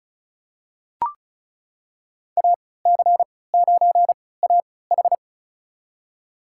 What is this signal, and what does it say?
Morse code, heard as a keyed tone, about 36 wpm, spaced as for 20 wpm, 700 Hz. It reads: AC9AH
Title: USA Call Signs - 5 Characters - 3x Character Spacing - 35wpm